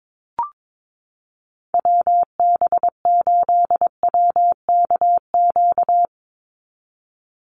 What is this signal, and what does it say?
Morse code, heard as a keyed tone, about 22 wpm, 700 Hz. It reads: WB8WKQ
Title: Call Signs - 22wpm